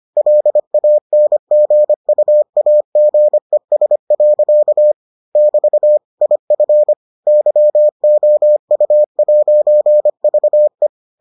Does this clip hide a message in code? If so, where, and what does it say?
Yes: Morse code from 0.2 to 10.9 s, heard as a keyed tone, about 25 wpm, 600 Hz. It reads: LANGUAGES. <BT> IF YOU'VE